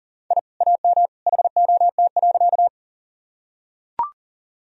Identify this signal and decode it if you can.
Morse code, heard as a keyed tone, about 40 wpm, 700 Hz. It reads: I AM HOT.
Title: Sentences from Top 1000 Words - Review Part 3 - 40wpm